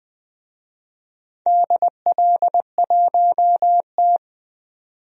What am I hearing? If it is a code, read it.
Morse code, heard as a keyed tone, about 20 wpm, 700 Hz. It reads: DL1T
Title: DXCC-100 Call Signs - 4 Characters - 20wpm